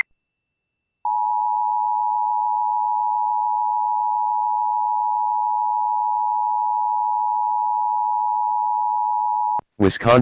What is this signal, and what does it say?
Emergency Alert System (SAME) tones (the 853 and 960 Hz attention tone), with speech. It is heard from 1.0 to 9.6 s.